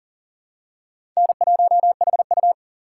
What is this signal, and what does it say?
Morse code, heard as a keyed tone, about 40 wpm, 700 Hz. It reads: N1HU
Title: USA Call Signs - 4 Characters - 2x Character Spacing - 40wpm